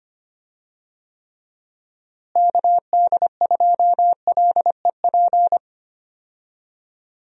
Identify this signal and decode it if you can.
Morse code, heard as a keyed tone, about 25 wpm, 700 Hz. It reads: KD2LEP
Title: USA Call Signs - 6 Characters - Repeated 3x - 25wpm